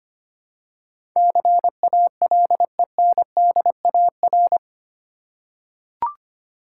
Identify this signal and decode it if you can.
Morse code, heard as a keyed tone, about 25 wpm, 690 Hz. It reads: CALENDAR
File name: practice / 25wpm / Easy Words with Hints - 25wpm